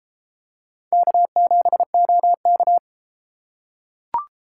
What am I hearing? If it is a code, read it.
Morse code, heard as a keyed tone, about 29 wpm, 700 Hz. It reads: K7OK